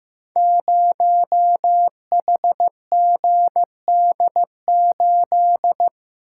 Morse code, heard as a keyed tone, about 15 wpm, 700 Hz. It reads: 0HGD8
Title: Code Groups - 5 Letter:Numbers - 15wpm